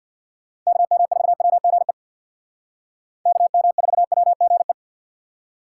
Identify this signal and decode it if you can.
Morse code, heard as a keyed tone, about 64 wpm, 700 Hz. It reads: KM4WGE KM4WGE